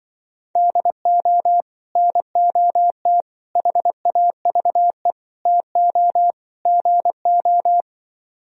Morse code, heard as a keyed tone, about 24 wpm, 700 Hz. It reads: DO NOT HAVE TO GO
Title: Speed-Racing - Sets of 5 Words 20wpm